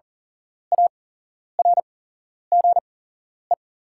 Morse code, heard as a keyed tone, about 40 wpm, spaced as for 11 wpm, 690 Hz. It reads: ARGE
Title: 8x Character Spacing - Top 300 Words - 40wpm